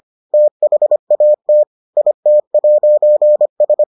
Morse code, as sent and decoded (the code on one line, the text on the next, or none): - .... .- - / .. - .----. ...
THAT IT'S